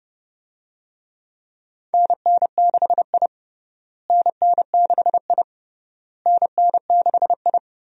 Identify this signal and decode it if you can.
Morse code, heard as a keyed tone, about 30 wpm, 700 Hz. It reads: NN6I NN6I NN6I
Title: USA Call Signs - 4 Characters - Repeated 3x - 30wpm